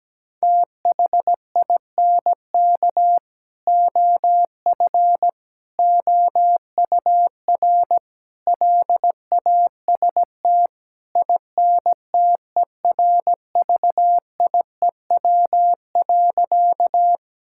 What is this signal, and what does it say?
Morse code, heard as a keyed tone, about 17 wpm, 700 Hz. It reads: THINK OF OUR LAST INTERVIEW.
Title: Sentences from Top 1600 Words 17wpm